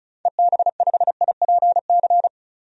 Morse code, heard as a keyed tone, about 36 wpm, 700 Hz. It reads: EB5IPC
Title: DXCC-100 Call Signs - 6 Characters - 35wpm